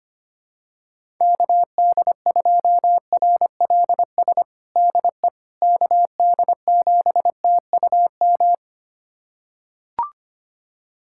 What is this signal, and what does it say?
Morse code, heard as a keyed tone, about 25 wpm, 700 Hz. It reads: KD2RLS DE KD7TUM
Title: USA Callsign DE Callsign - 6 Characters - 25wpm